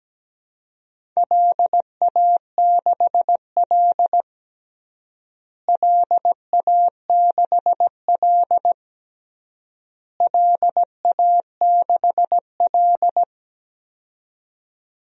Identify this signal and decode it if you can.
Morse code, heard as a keyed tone, about 17 wpm, 700 Hz. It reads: LA6L LA6L LA6L
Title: DXCC-100 Call Signs - 4 Characters - Repeated 3x - 17wpm